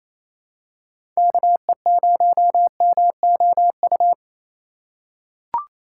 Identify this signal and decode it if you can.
Morse code, heard as a keyed tone, about 28 wpm, 700 Hz. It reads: KE0MOU
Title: USA Call Signs - 6 Characters - 4x Character Spacing - 28wpm